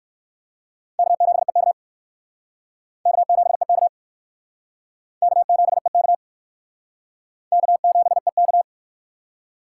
Morse code, heard as a keyed tone, about 55 wpm, 700 Hz. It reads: K7EK K7EK K7EK K7EK
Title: Speed Racing - USA 1x2 Call Signs 45wpm